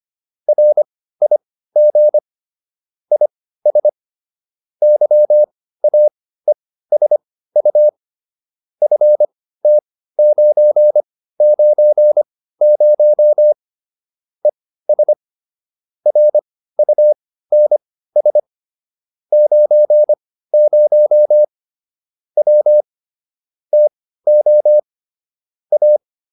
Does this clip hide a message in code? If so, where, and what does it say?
Yes: Morse code from 0.5 to 26.0 s, heard as a keyed tone, about 25 wpm, spaced as for 15 wpm, 600 Hz. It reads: RIG IS YAESU FT990 ES RUNS 90 W TO A